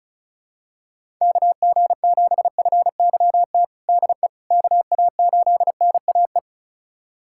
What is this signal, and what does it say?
Morse code, heard as a keyed tone, about 34 wpm, 700 Hz. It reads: KG7FYT DE KA8NAE
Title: USA Callsign DE Callsign - 6 Characters - 35wpm